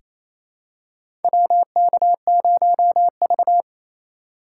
Morse code, heard as a keyed tone, about 28 wpm, 700 Hz. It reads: WK0V
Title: USA Call Signs - 4 Characters - 2x Character Spacing - 28wpm